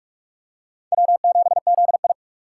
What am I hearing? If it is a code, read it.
Morse code, heard as a keyed tone, about 44 wpm, 700 Hz. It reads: W7ZI
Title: USA Call Signs - 4 Characters - 6x Character Spacing - 45wpm